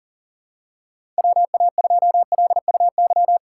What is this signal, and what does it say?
Morse code, heard as a keyed tone, about 40 wpm, 700 Hz. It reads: WA2LUY
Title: USA Call Signs - 6 Characters - 2x Character Spacing - 40wpm